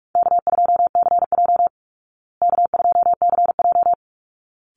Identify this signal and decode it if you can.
Morse code, heard as a keyed tone, about 44 wpm, 700 Hz. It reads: K2CJ K2CJ
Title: Call Signs Repeated 3x - 45wpm